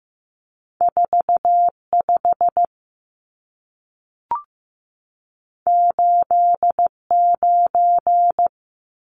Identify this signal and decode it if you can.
Morse code, heard as a keyed tone, about 15 wpm, 700 Hz. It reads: 45 89
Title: Numbers - 2-Digit 15wpm